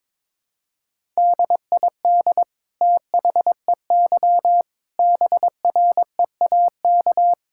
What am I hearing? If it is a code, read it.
Morse code, heard as a keyed tone, about 22 wpm, 700 Hz. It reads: DID THEY BREAK
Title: Sentences from Top 1000 Words - Review Part 6 - 22wpm